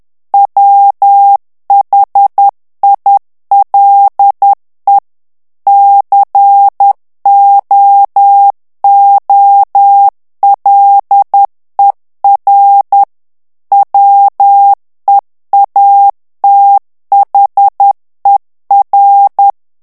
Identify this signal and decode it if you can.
Morse code, heard as a keyed tone, about 10 wpm, 800 Hz. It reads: WHILE COOLER WEATHER
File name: morse mp3 / ANGLAIS texte clair 2 vitesse 10 mots minute